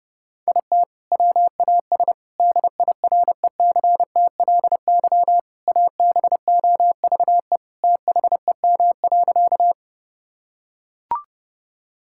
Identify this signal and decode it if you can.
Morse code, heard as a keyed tone, about 30 wpm, 700 Hz. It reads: IT WAS DIRECTLY ABOVE THEM.
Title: Sentences from Top 1000 Words - Review Part 5 - 30wpm